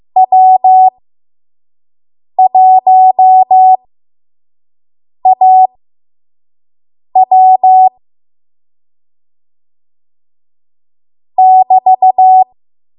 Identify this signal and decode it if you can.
Morse code, heard as a keyed tone, about 15 wpm, spaced as for 5 wpm, 750 Hz. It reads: W1AW <BT>
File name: Morse / 05WPM